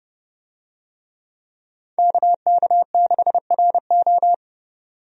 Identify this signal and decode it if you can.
Morse code, heard as a keyed tone, about 30 wpm, 700 Hz. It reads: KK6RO
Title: USA Call Signs - 5 Characters - 30wpm